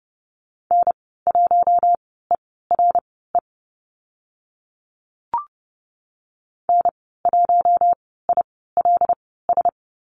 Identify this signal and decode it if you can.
Morse code, heard as a keyed tone, about 30 wpm, spaced as for 17 wpm, 700 Hz. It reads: N1ERE N1ILS
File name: Lesson 14g - Number 2 - Call Signs 30wpm